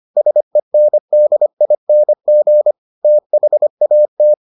Morse code, heard as a keyed tone, about 25 wpm, 600 Hz. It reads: SENDING THAT